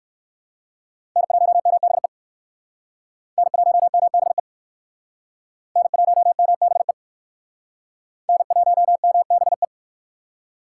Morse code, heard as a keyed tone, about 57 wpm, 700 Hz. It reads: N1MBE N1MBE N1MBE N1MBE